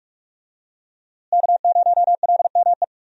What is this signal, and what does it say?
Morse code, heard as a keyed tone, about 44 wpm, 700 Hz. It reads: K0LME
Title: USA Call Signs - 5 Characters - Repeated 2x - 45wpm